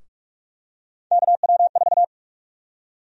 Morse code, heard as a keyed tone, about 45 wpm, 700 Hz. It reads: KW4
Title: USA Call Sign Prefixes - 45wpm